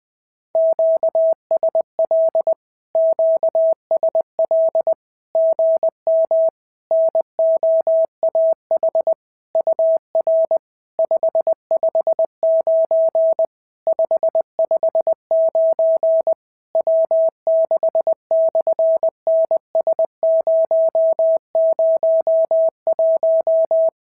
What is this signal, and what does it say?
Morse code, heard as a keyed tone, about 20 wpm, 690 Hz. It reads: QSL QSL GM NOAH UR 559 559 W6/NS001
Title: SOTA QSO Practice 20wpm